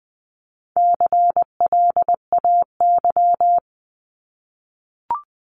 Morse code, heard as a keyed tone, about 20 wpm, 700 Hz. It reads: CLAY
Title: 260 Lesson 42e - Words 20-20wpm